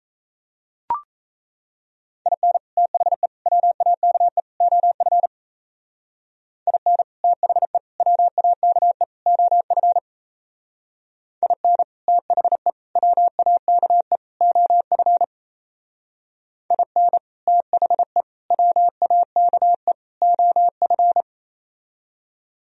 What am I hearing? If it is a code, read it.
Morse code, heard as a keyed tone, about 36 wpm, 700 Hz. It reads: IN THE WAKE OF IN THE WAKE OF IN THE WAKE OF IN THE WAKE OF